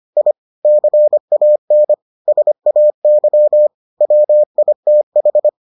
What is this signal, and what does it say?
Morse code, heard as a keyed tone, about 25 wpm, 600 Hz. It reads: I CAN SAY WITH